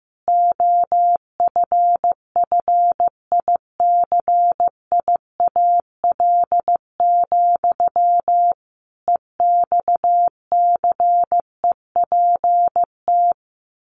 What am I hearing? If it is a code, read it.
Morse code, heard as a keyed tone, about 15 wpm, 700 Hz. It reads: OFFICIAL, EXCEPT
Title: A Sound Of Thunder 15wpm